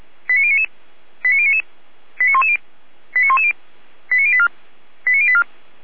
Five-tone selective call (ZVEI), heard as quick runs of five tones, tones of about 70 ms. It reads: 89099 89099 89110 89110 89084 89084